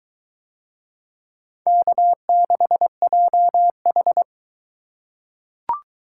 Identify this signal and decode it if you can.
Morse code, heard as a keyed tone, about 20 wpm, 700 Hz. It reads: K6JH